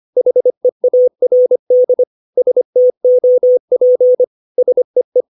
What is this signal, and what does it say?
Morse code, heard as a keyed tone, about 25 wpm, 500 Hz. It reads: HEARD STOP SEE
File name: 632 words / 632 Words 25WPM